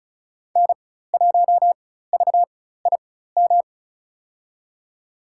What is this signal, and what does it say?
Morse code, heard as a keyed tone, about 36 wpm, spaced as for 16 wpm, 700 Hz. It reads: N1VIM
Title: USA Call Signs - 5 Characters - 4x Character Spacing - 35wpm